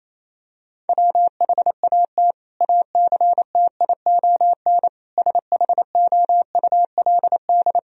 Morse code, heard as a keyed tone, about 28 wpm, 700 Hz. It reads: WHAT ACTION SHOULD
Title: Sentences from Top 1000 Words - Review Part 4 - 28wpm